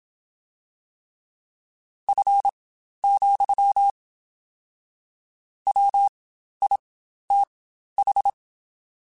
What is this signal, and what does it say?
Morse code, heard as a keyed tone, about 27 wpm, spaced as for 12 wpm, 800 Hz. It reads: F, WITH
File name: Spaced Characters / Spaced Characters-mp3s / Weather4 14 sp @25